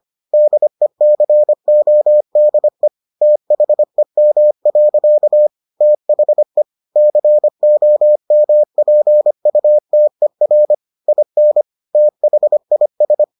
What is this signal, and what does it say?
Morse code, heard as a keyed tone, about 25 wpm, 600 Hz. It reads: DECODE THEM. THE COMPUTER IN THIS